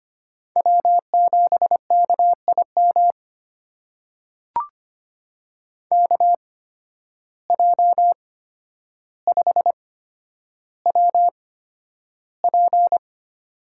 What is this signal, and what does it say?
Morse code, heard as a keyed tone, about 25 wpm, spaced as for 11 wpm, 700 Hz. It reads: W7KIM KJ5WP
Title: USA Call Signs - 5 Characters - 8x Character Spacing - 25wpm